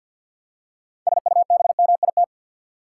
Morse code, heard as a keyed tone, about 51 wpm, 700 Hz. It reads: SUBMIT
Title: Six Letter Words - 50wpm